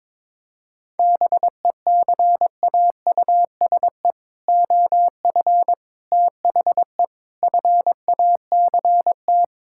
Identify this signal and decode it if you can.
Morse code, heard as a keyed tone, about 22 wpm, 700 Hz. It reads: BECAUSE OF THE FACT
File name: practice / 22wpm / Sentences with 4-Word Phrases - 22wpm